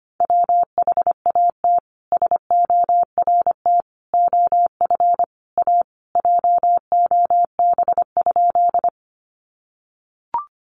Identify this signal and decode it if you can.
Morse code, heard as a keyed tone, about 25 wpm, 700 Hz. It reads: WHAT SORT OF A JOB?
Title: Sentences from Top 1300 Words 25wpm